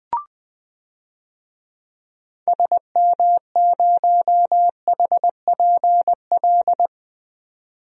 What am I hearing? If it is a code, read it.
Morse code, heard as a keyed tone, about 20 wpm, 700 Hz. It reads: SM0HPL
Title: DXCC-100 Call Signs - 6 Characters - 20wpm